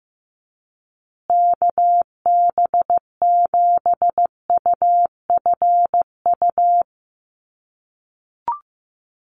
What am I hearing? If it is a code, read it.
Morse code, heard as a keyed tone, about 15 wpm, 700 Hz. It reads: KB7UFU
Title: USA Call Signs Repeated 3x 15wpm